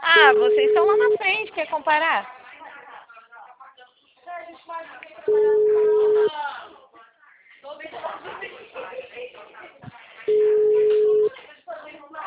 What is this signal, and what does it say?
Ringback tone (Europe, 425 Hz), with speech.
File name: out- -2025- - - .46944